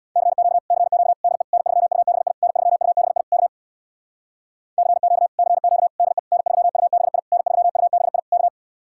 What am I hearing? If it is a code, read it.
Morse code, heard as a keyed tone, about 66 wpm, 700 Hz. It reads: CQ CQ DE N3UBI N3UBI K CQ CQ DE N3UBI N3UBI K